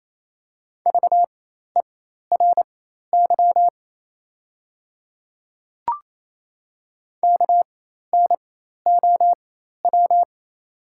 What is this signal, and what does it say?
Morse code, heard as a keyed tone, about 27 wpm, spaced as for 13 wpm, 690 Hz. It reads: VERY KNOW
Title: 4x Character Spacing - Top 300 Words - 28wpm